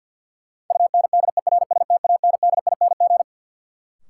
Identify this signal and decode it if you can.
Morse code, heard as a keyed tone, about 49 wpm, 700 Hz. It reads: UNDERSTANDING
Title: Sentences from Top 1400 Words - New Word Drill Alt - 50wpm